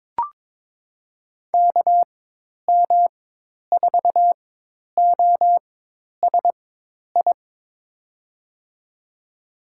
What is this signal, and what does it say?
Morse code, heard as a keyed tone, about 21 wpm, spaced as for 10 wpm, 700 Hz. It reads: KM4OSI
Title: USA Call Signs - 6 Characters - 4x Character Spacing - 22wpm